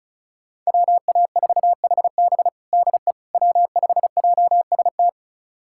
Morse code, heard as a keyed tone, about 36 wpm, 700 Hz. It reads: WA4HB DE W5JST
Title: USA Callsign DE Callsign - 5 Characters - 35wpm